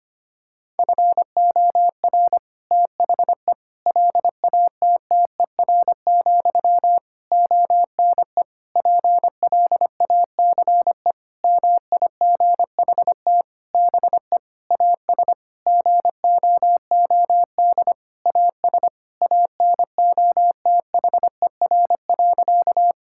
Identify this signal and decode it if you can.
Morse code, heard as a keyed tone, about 25 wpm, 700 Hz. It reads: FOR THE LATTER, ONE PLACE MIGHT BE AS GOOD AS ANOTHER.
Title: Sentences from Top 1600 Words 25wpm